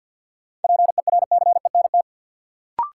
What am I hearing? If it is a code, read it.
Morse code, heard as a keyed tone, about 53 wpm, 700 Hz. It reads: PERCENT
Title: Sets of 2 Words - New Word Drill - 50wpm